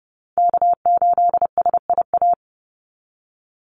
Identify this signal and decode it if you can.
Morse code, heard as a keyed tone, about 30 wpm, 700 Hz. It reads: K8SIA
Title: Call Signs Repeated 3x - 30wpm